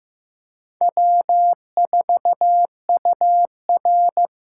Morse code, heard as a keyed tone, about 18 wpm, 700 Hz. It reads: W4UR
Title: Speed Racing - USA Call Signs 15wpm